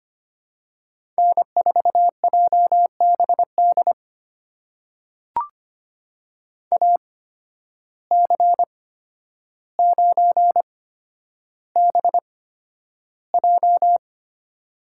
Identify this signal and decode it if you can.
Morse code, heard as a keyed tone, about 25 wpm, spaced as for 11 wpm, 700 Hz. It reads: N4JBD AC9BJ
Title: USA Call Signs - 5 Characters - 8x Character Spacing - 25wpm